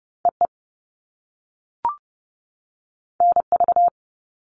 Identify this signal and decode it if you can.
Morse code, heard as a keyed tone, about 30 wpm, 700 Hz. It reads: EE NV
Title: All Cut Numbers - 2-Digit 30wpm